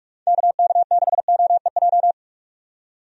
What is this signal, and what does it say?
Morse code, heard as a keyed tone, about 45 wpm, 700 Hz. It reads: KK6OEJ
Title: USA Call Signs - 6 Characters - 45wpm